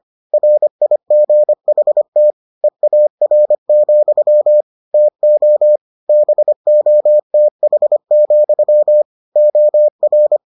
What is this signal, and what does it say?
Morse code, heard as a keyed tone, about 25 wpm, 600 Hz. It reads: RIGHT EAR, TO BOTH, OR